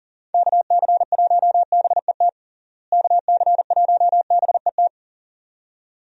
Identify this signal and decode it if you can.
Morse code, heard as a keyed tone, about 40 wpm, 700 Hz. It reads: KC1BET KC1BET
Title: USA Call Signs - 6 Characters - Repeated 2x - 40wpm